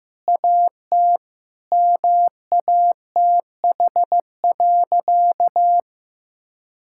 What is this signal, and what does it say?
Morse code, heard as a keyed tone, about 15 wpm, 700 Hz. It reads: AT MATH.
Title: Sentences from Top 1000 Words - Review Part 6 - 15wpm